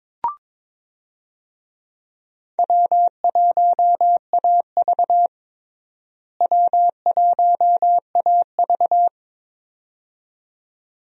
Morse code, heard as a keyed tone, about 22 wpm, 700 Hz. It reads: W1AV W1AV
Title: USA Call Signs - 4 Characters - Repeated 2x - 22wpm